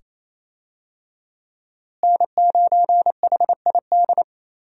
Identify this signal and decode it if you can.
Morse code, heard as a keyed tone, about 28 wpm, 700 Hz. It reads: N9HID
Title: USA Call Signs - 5 Characters - Repeated 2x - 28wpm